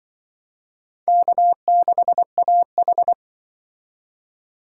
Morse code, heard as a keyed tone, about 23 wpm, 700 Hz. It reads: K6AH
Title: Speed Racing - USA 1x2 Call Signs 20wpm